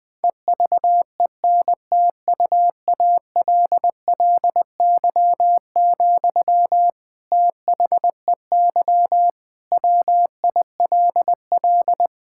Morse code, heard as a keyed tone, about 19 wpm, 700 Hz. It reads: EVENTUALLY, THEY WILL
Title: Sentences from Top 1000 Words - Review Part 6 - 20wpm